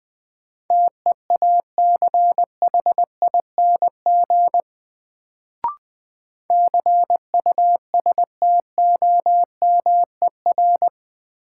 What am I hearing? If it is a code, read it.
Morse code, heard as a keyed tone, about 20 wpm, 700 Hz. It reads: TEACHING CUSTOMER
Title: Sentences from Top 1100 Words - New Word Drill Alt - 20wpm